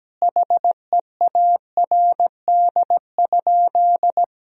Morse code, heard as a keyed tone, about 17 wpm, 700 Hz. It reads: HEARD?
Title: Sentences from Top 400 Words and Abbreviations 17wpm